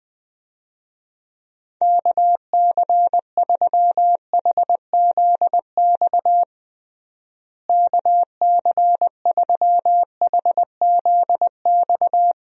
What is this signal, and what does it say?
Morse code, heard as a keyed tone, about 20 wpm, 700 Hz. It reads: KC3HZX KC3HZX
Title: USA Call Signs - 6 Characters - Repeated 3x - 20wpm